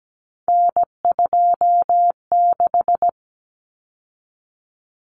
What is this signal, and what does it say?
Morse code, heard as a keyed tone, about 17 wpm, 700 Hz. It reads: N26
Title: Common Cut Numbers - 3-Digit 17wpm